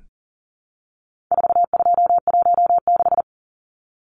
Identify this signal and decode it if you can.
Morse code, heard as a keyed tone, about 40 wpm, 700 Hz. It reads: 4216
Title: Numbers - 4-Digit 40wpm